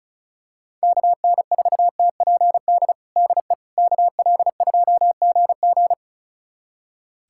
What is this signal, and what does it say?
Morse code, heard as a keyed tone, about 34 wpm, 700 Hz. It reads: KN4TPD DE KL2GG
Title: USA Callsign DE Callsign 35wpm